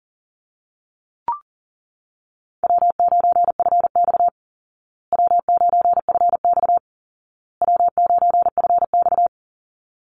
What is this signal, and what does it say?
Morse code, heard as a keyed tone, about 40 wpm, 700 Hz. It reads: W9FX W9FX W9FX
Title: Easy Call Signs Repeated 3x - 40wpm